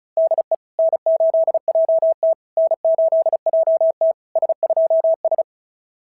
Morse code, heard as a keyed tone, about 34 wpm, 650 Hz. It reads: DE N8JT N8JT S2S
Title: SOTA QSO Practice 35wpm